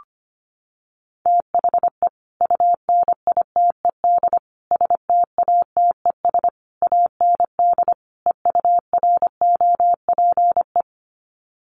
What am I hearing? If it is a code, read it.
Morse code, heard as a keyed tone, about 25 wpm, 700 Hz. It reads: THE UNITED STATES AND EUROPE
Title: Sets of 5 Words Encore 25wpm